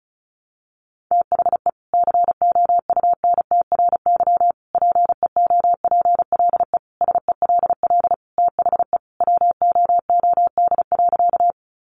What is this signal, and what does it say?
Morse code, heard as a keyed tone, about 34 wpm, 700 Hz. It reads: THE COUNTRY PEOPLE SELL THE WOOD.
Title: Sentences from Top 1100 Words 35wpm